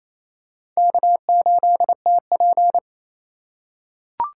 Morse code, heard as a keyed tone, about 28 wpm, 700 Hz. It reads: K8TP
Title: USA Call Signs - 4x Character Spacing - 28wpm